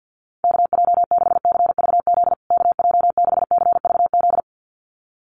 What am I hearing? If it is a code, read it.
Morse code, heard as a keyed tone, about 50 wpm, 700 Hz. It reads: KJ6CVZ KJ6CVZ
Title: USA Call Signs Repeated 2x 50wpm